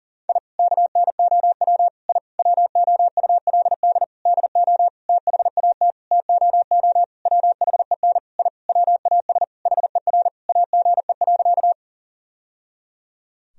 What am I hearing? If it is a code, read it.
Morse code, heard as a keyed tone, about 37 wpm, 700 Hz. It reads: I KNOW I WOULD DO THAT TOO WHEN I WAS HER AGE.
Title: Sentences from Top 1000 Words - Review Part 3 - 40wpm